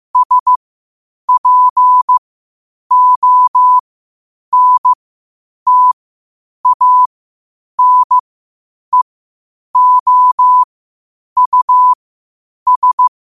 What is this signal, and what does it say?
Morse code, heard as a keyed tone, about 15 wpm, spaced as for 8 wpm, 1000 Hz. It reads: SPONTANEOUS
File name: qotdfiles / 8wpm